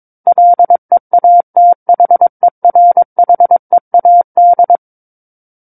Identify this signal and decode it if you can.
Morse code, heard as a keyed tone, about 22 wpm, 700 Hz. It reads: LEATHERHEAD
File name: Book One / 13-Chapter 13 - 22 WPM